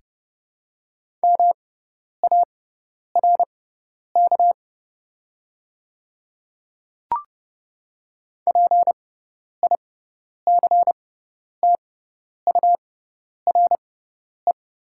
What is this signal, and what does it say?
Morse code, heard as a keyed tone, about 30 wpm, spaced as for 10 wpm, 700 Hz. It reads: MARK PICTURE
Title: 6x Character Spacing - Top 300 Words - 30wpm